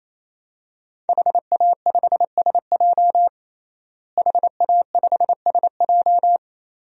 Morse code, heard as a keyed tone, about 28 wpm, 700 Hz. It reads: HA5SJ HA5SJ
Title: DXCC-100 Call Signs - 5 Characters - Repeated 3x - 28wpm